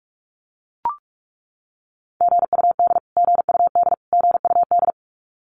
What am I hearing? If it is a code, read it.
Morse code, heard as a keyed tone, about 44 wpm, 700 Hz. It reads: GUD GUD GUD
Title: Abbreviations Repeated 3x - 45wpm